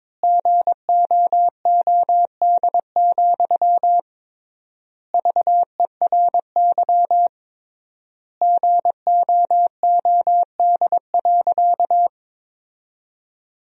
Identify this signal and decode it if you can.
Morse code, heard as a keyed tone, about 22 wpm, 700 Hz. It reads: GOOD, VERY GOOD.
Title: 3x Word-Spacing - Sentences from Top 200 Words - 22wpm